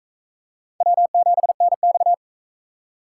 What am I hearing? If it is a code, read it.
Morse code, heard as a keyed tone, about 44 wpm, 700 Hz. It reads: W7NX